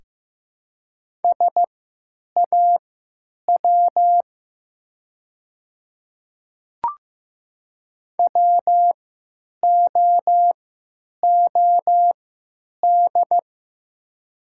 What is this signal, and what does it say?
Morse code, heard as a keyed tone, about 15 wpm, spaced as for 9 wpm, 700 Hz. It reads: SAW WOOD
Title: 3x Character Spacing - Top 300 Words - 15wpm